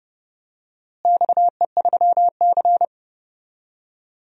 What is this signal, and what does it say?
Morse code, heard as a keyed tone, about 30 wpm, 700 Hz. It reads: XE3C
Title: DXCC-100 Call Signs - 4 Characters - Repeated 3x - 30wpm